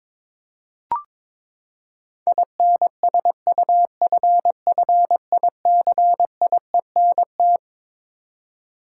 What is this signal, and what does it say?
Morse code, heard as a keyed tone, about 22 wpm, 700 Hz. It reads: INSUFFICIENT
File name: Twelve Letter Words - 22wpm